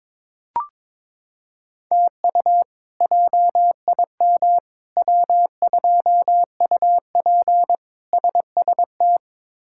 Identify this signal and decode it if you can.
Morse code, heard as a keyed tone, about 22 wpm, 700 Hz. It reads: TU JIM W2UP SST
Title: K1USN SST QSO Practice - 22wpm